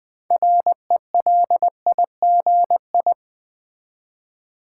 Morse code, heard as a keyed tone, about 20 wpm, 700 Hz. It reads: RELIGI
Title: Build A Word - 2 Letters At A Time - 8-Letter Words - 20wpm